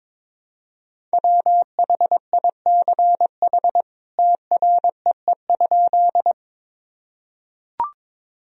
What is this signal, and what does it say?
Morse code, heard as a keyed tone, about 22 wpm, 700 Hz. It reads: WHICH TREE?
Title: Sentences from Top 1000 Words - Review Part 1 - 22wpm